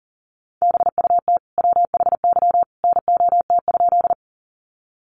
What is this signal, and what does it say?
Morse code, heard as a keyed tone, about 40 wpm, 700 Hz. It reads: BUT WHY NOT?
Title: Sentences from Top 600 Words 40wpm